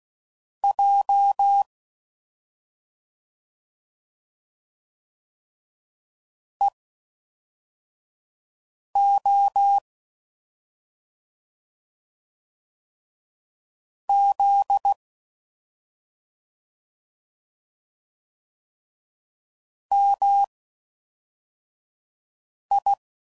Morse code, heard as a keyed tone, about 16 wpm, spaced as for 4 wpm, 800 Hz. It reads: J EO Z MI
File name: Mixed / CW Comparisons / CW Comparisons Broken Characters